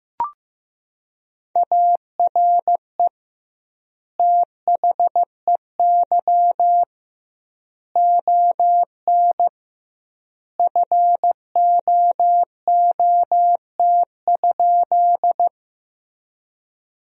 Morse code, heard as a keyed tone, about 15 wpm, 700 Hz. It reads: ARE THEY ON FOOT?
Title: 2x Word-Spacing - Sentences from Top 500 Words - 15wpm